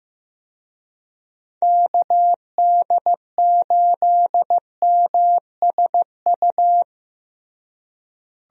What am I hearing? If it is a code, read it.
Morse code, heard as a keyed tone, about 15 wpm, 700 Hz. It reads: KD8MSU
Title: USA Call Signs - 6 Characters - Repeated 3x - 15wpm